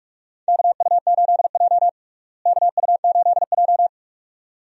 Morse code, heard as a keyed tone, about 46 wpm, 700 Hz. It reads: KU8J KU8J
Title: USA Call Signs - 4 Characters - Repeated 2x - 45wpm